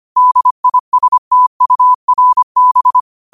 Morse code, heard as a keyed tone, about 25 wpm, 1000 Hz. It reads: DISTURB